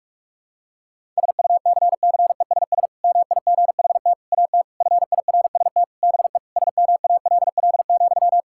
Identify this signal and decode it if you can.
Morse code, heard as a keyed tone, about 46 wpm, 700 Hz. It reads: SUCCESS MIGHT AT FIRST BE SMALL,
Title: Sentences from Top 1000 Words - Review Part 6 - 45wpm